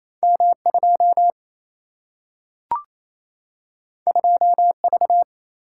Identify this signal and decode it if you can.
Morse code, heard as a keyed tone, about 28 wpm, 700 Hz. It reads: M 2 2 V
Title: Single-Letter Single-Number 28wpm